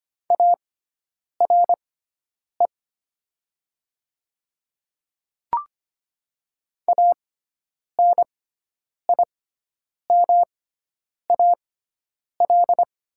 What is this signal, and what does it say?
Morse code, heard as a keyed tone, about 25 wpm, spaced as for 9 wpm, 700 Hz. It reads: ARE ANIMAL
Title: 6x Character Spacing - Top 200 Words - 25wpm